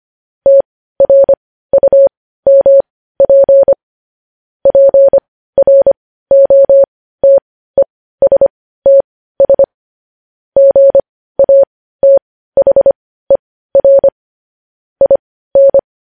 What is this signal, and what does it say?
Morse code, heard as a keyed tone, about 25 wpm, spaced as for 15 wpm, 550 Hz. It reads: TRUMP PROTESTS GATHER IN
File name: podcast / News headlines-1x-25x15